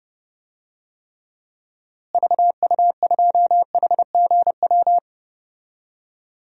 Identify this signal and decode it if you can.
Morse code, heard as a keyed tone, about 30 wpm, 700 Hz. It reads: VU2HGW
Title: DXCC-100 Call Signs - 6 Characters - Repeated 2x - 30wpm